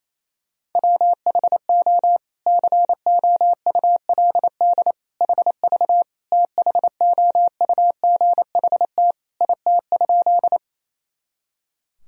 Morse code, heard as a keyed tone, about 28 wpm, 700 Hz. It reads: WHO COULD HV THOUGHT IT?
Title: Sentences from Top 300 Words and Abbreviations 28wpm